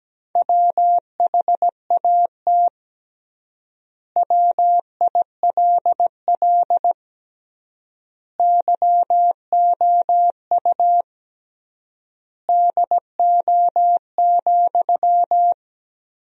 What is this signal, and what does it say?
Morse code, heard as a keyed tone, about 17 wpm, 700 Hz. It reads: WHAT WILL YOU DO,
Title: 3x Word-Spacing - Sentences from Top 100 Words - 17wpm